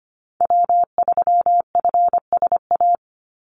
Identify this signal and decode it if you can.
Morse code, heard as a keyed tone, about 25 wpm, 700 Hz. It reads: W3FSA
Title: Call Signs - 25wpm